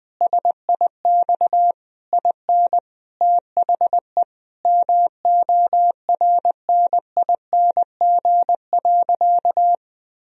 Morse code, heard as a keyed tone, about 20 wpm, 700 Hz. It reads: SIX IN THE MORNING.
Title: Sentences from Top 1000 Words - Review Part 2 - 20wpm